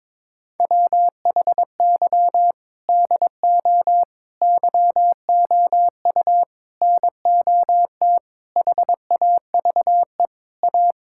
Morse code, heard as a keyed tone, about 22 wpm, 700 Hz. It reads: WHY DO YOU NOT HAVE A
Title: Sentences from Top 1600 Words 22wpm